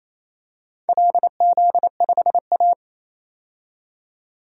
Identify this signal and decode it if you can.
Morse code, heard as a keyed tone, about 28 wpm, 700 Hz. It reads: LZ5A
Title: DXCC-100 Call Signs - 4 Characters - 28wpm